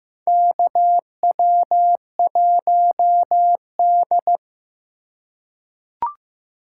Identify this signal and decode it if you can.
Morse code, heard as a keyed tone, about 15 wpm, 700 Hz. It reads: KW1D
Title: USA Call Signs - 4 Characters - 3x Character Spacing - 15wpm